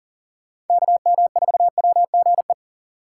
Morse code, heard as a keyed tone, about 51 wpm, 700 Hz. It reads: KM4WGE